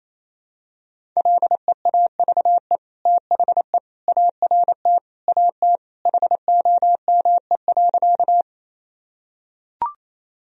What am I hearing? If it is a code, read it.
Morse code, heard as a keyed tone, about 28 wpm, 700 Hz. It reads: LEAVE THE ART AT HOME.
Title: Sentences from Top 1000 Words - Review Part 3 - 28wpm